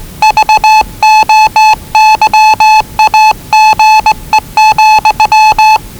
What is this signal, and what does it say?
Morse code, heard as a keyed tone, about 20 wpm, 840 Hz. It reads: VOYAGE,
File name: WellingtonRadioZlwCloseDownOnHfAndMf / 01-Zlw 4277khz 417.5 qrt until 0925z